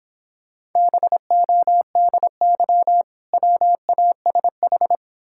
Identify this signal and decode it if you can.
Morse code, heard as a keyed tone, about 26 wpm, 700 Hz. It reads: BODY WASH